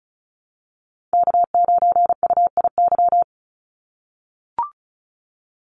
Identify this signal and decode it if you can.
Morse code, heard as a keyed tone, about 36 wpm, 700 Hz. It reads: K9UIY
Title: Call Signs - 35wpm